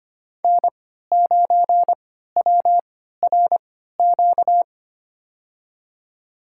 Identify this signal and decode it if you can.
Morse code, heard as a keyed tone, about 25 wpm, spaced as for 14 wpm, 700 Hz. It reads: N9WRQ
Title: USA Call Signs - 5 Characters - 3x Character Spacing - 25wpm